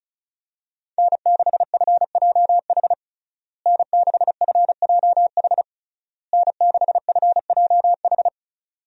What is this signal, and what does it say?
Morse code, heard as a keyed tone, about 36 wpm, 700 Hz. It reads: N6FJH N6FJH N6FJH
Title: USA Call Signs - 5 Characters - Repeated 3x - 35wpm